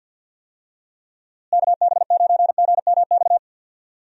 Morse code, heard as a keyed tone, about 52 wpm, 700 Hz. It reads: KB9GMX
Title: USA Call Signs - 6 Characters - Repeated 3x - 50wpm